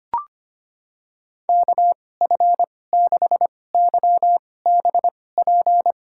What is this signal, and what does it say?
Morse code, heard as a keyed tone, about 25 wpm, spaced as for 18 wpm, 700 Hz. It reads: KF6YBP
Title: USA Call Signs - 2x Character Spacing - 25wpm